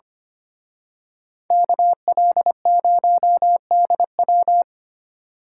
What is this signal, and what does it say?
Morse code, heard as a keyed tone, about 29 wpm, 700 Hz. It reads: KL0DW
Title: Speed Racing - USA Call Signs 25wpm